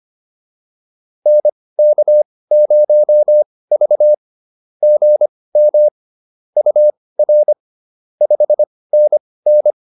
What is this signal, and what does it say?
Morse code, heard as a keyed tone, about 25 wpm, spaced as for 18 wpm, 600 Hz. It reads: NK0V GM UR 5NN
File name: POTA101 18